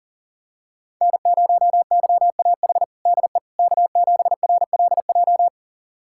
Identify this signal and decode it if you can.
Morse code, heard as a keyed tone, about 40 wpm, 700 Hz. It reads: N0YAH DE K7RLJ
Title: USA Callsign DE Callsign - 5 Characters - 40wpm